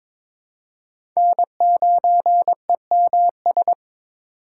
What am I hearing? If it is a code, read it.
Morse code, heard as a keyed tone, about 22 wpm, 700 Hz. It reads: N9EMS
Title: USA Call Signs - 5 Characters - 2x Character Spacing - 22wpm